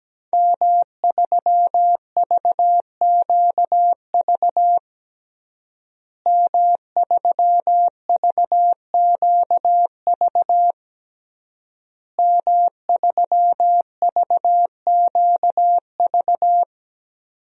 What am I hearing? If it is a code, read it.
Morse code, heard as a keyed tone, about 17 wpm, 700 Hz. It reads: M3VQV M3VQV M3VQV
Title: DXCC-100 Call Signs - 5 Characters - Repeated 3x - 17wpm